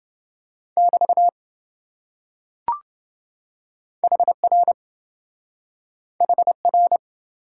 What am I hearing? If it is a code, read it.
Morse code, heard as a keyed tone, about 27 wpm, 700 Hz. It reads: <BT> HR HR